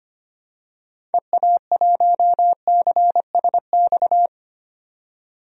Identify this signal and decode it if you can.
Morse code, heard as a keyed tone, about 25 wpm, 700 Hz. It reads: EA1CSX
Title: DXCC-100 Call Signs - 6 Characters - 25wpm